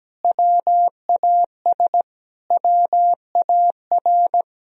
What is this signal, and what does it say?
Morse code, heard as a keyed tone, about 17 wpm, 700 Hz. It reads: WAS WAR
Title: Sentences from Top 1000 Words - Review Part 2 - 17wpm